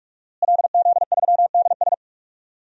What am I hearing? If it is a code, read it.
Morse code, heard as a keyed tone, about 46 wpm, 700 Hz. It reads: LZ3DS
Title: DXCC-100 Call Signs - 5 Characters - Repeated 2x - 45wpm